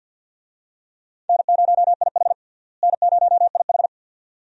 Morse code, heard as a keyed tone, about 50 wpm, 700 Hz. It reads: N0IH N0IH
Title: USA Call Signs - 4 Characters - Repeated 2x - 50wpm